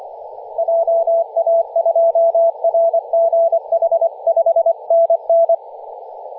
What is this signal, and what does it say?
Morse code, heard as a keyed tone, about 25 wpm, 670 Hz, with radio noise. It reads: JA2RGH 5NN
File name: posts / 2016-nov-10-0528z-t30ma-21-007mhz-cw